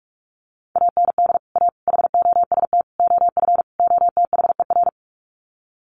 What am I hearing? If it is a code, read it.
Morse code, heard as a keyed tone, about 44 wpm, 700 Hz. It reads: AND A HOST OF OTHER
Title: Sets of 5 Words Encore2 45wpm